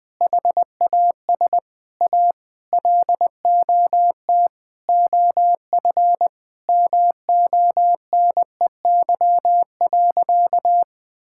Morse code, heard as a keyed tone, about 20 wpm, 700 Hz. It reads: HAS A LOT OF MONEY.
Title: Sentences from Top 1000 Words - Review Part 5 - 20wpm